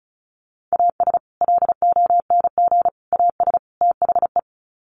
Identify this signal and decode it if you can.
Morse code, heard as a keyed tone, about 34 wpm, 700 Hz. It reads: AS LONG AS THE
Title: Sets of 4 Words 35wpm